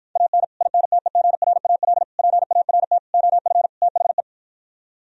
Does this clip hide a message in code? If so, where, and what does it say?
Yes: Morse code from 0.2 to 4.2 s, heard as a keyed tone, about 52 wpm, 700 Hz. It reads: AN INTEGRAL PART OF THE